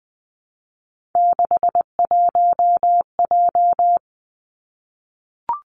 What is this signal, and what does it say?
Morse code, heard as a keyed tone, about 20 wpm, 700 Hz. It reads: 61J
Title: Three Challenging Letter-Number Combinations 20wpm